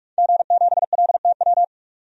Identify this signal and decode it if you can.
Morse code, heard as a keyed tone, about 45 wpm, 700 Hz. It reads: G7LTW
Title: DXCC-100 Call Signs - 5 Characters - 45wpm